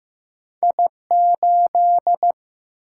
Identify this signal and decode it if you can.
Morse code, heard as a keyed tone, about 15 wpm, 700 Hz. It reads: I8
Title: Single-Letter Single-Number 15wpm